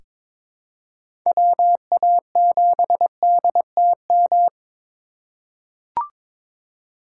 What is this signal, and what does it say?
Morse code, heard as a keyed tone, about 22 wpm, 700 Hz. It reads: WA7DTM
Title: USA Call Signs - 6 Characters - 3x Character Spacing - 22wpm